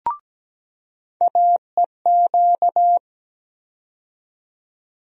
Morse code, heard as a keyed tone, about 17 wpm, 700 Hz. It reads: AEQ
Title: USA Call Sign Postfixes - 3 Letters 17wpm